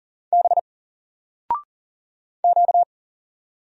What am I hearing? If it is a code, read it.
Morse code, heard as a keyed tone, about 41 wpm, 700 Hz. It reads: B Q
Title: Single Character (Letter:Number:Punctuation:Prosign) Alt 40wpm